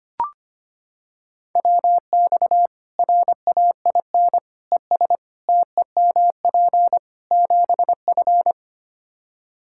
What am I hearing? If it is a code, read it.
Morse code, heard as a keyed tone, about 25 wpm, 700 Hz. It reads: WX RAIN ES TEMP 7F
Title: Practice QSO - Exchange 2 - Info - 25wpm